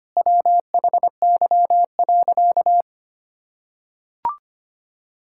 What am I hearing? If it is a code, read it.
Morse code, heard as a keyed tone, about 25 wpm, 700 Hz. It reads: WHY.
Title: Sentences from Top 1000 Words - Review Part 6 - 25wpm